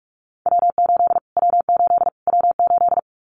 Morse code, heard as a keyed tone, about 44 wpm, 700 Hz. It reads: W8 W8 W8
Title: Callsign Prefixes for Top 25 DXCC - Repeated 3x - 45wpm